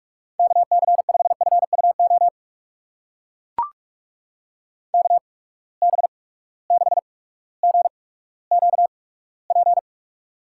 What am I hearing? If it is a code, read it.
Morse code, heard as a keyed tone, about 44 wpm, spaced as for 19 wpm, 700 Hz. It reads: KC5FUO KB6GQP
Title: USA Call Signs - 6 Characters - 8x Character Spacing - 45wpm